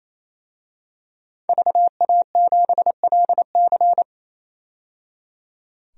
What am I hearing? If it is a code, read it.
Morse code, heard as a keyed tone, about 28 wpm, 700 Hz. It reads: VA7LC
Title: DXCC-100 Call Signs - 5 Characters - 28wpm